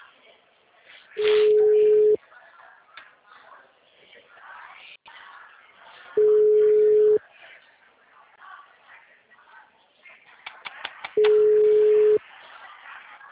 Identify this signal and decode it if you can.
Ringback tone (Europe, 425 Hz), with speech.
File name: out- -2005- - - .37893